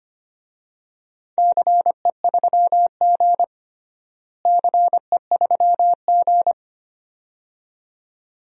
Morse code, heard as a keyed tone, about 25 wpm, 700 Hz. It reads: CE3G CE3G
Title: DXCC-100 Call Signs - 4 Characters - Repeated 2x - 25wpm